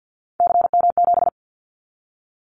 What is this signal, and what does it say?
Morse code, heard as a keyed tone, about 50 wpm, 700 Hz. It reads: CM7
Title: Callsign Prefixes for Top 100 DXCC - 50wpm